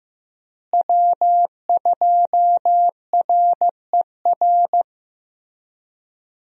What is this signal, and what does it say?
Morse code, heard as a keyed tone, about 15 wpm, 700 Hz. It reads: W2RER
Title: USA Call Signs - 5 Characters - Repeated 2x - 15wpm